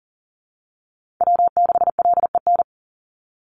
Morse code, heard as a keyed tone, about 40 wpm, 700 Hz. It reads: W6LEN
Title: Call Signs Repeated 3x - 40wpm